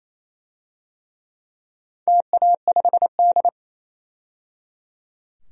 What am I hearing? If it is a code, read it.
Morse code, heard as a keyed tone, about 27 wpm, 700 Hz. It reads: TA5D
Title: DXCC-100 Call Signs - 4 Characters - 28wpm